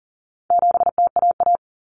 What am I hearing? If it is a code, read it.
Morse code, heard as a keyed tone, about 40 wpm, 700 Hz. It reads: 7TAA
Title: Common Cut Numbers - 4-Digit 40wpm